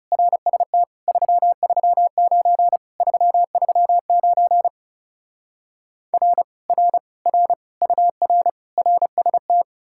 Morse code, heard as a keyed tone, about 40 wpm, 700 Hz. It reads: RST 339 339 R R R UR RST